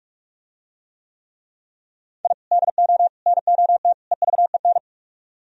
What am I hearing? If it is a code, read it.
Morse code, heard as a keyed tone, about 44 wpm, 700 Hz. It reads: I DO NOT EVEN